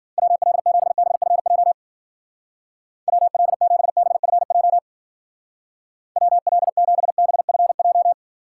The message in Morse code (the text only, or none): WL7BFJ WL7BFJ WL7BFJ